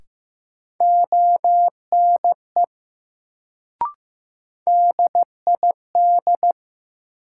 Morse code, heard as a keyed tone, about 15 wpm, 700 Hz. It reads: ONE DID
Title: Sentences from Top 100 Words - New Word Drill - 15wpm